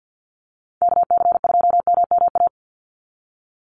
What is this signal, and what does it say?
Morse code, heard as a keyed tone, about 50 wpm, 700 Hz. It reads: KC2MMA
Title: USA Call Signs 50wpm